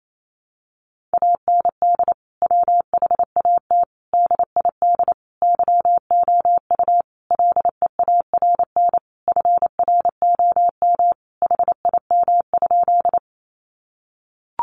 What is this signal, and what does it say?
Morse code, heard as a keyed tone, about 28 wpm, 700 Hz. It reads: AND WHAT DID YOU LEARN FROM HIM?
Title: Sentences from Top 300 Words 28wpm